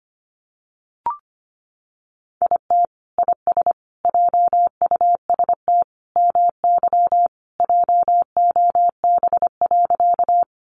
Morse code, heard as a keyed tone, about 25 wpm, 700 Hz. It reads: IT IS JUST MY JOB.
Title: Sentences from Top 600 Words 25wpm